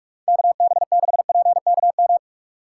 Morse code, heard as a keyed tone, about 48 wpm, 690 Hz. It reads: KB6PKM
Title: USA Call Signs - 4x Character Spacing - 45wpm